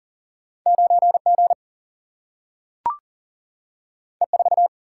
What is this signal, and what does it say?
Morse code, heard as a keyed tone, about 40 wpm, 700 Hz. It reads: 9G E4
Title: Single-Letter Single-Number 40wpm